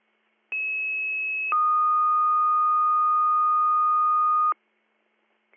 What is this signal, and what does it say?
Two-tone page, heard as a one-second tone, then a three-second tone, about 2600 then 1230 Hz. It is heard from 0.5 to 4.5 s.